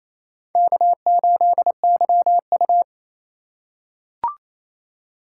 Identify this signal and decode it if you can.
Morse code, heard as a keyed tone, about 28 wpm, 700 Hz. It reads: K8YU
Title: USA Call Signs - 4 Characters - 2x Character Spacing - 28wpm